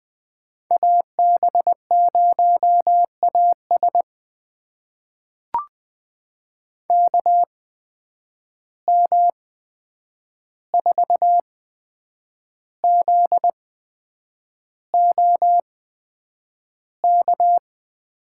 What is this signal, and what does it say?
Morse code, heard as a keyed tone, about 20 wpm, spaced as for 8 wpm, 700 Hz. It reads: AB0AS KM4ZOK
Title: USA Call Signs - 8x Character Spacing - 20wpm